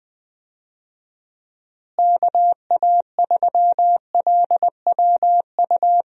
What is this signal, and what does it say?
Morse code, heard as a keyed tone, about 20 wpm, 700 Hz. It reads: KA3LWU
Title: USA Call Signs - 6 Characters - Repeated 2x - 20wpm